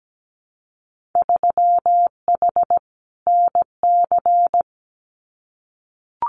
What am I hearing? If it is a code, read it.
Morse code, heard as a keyed tone, about 17 wpm, 700 Hz. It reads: 3H NC
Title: Winter Field Day Exchanges 17wpm